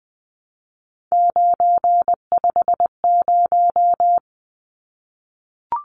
Morse code, heard as a keyed tone, about 20 wpm, 700 Hz. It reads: 950
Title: Numbers - 3-Digit 20wpm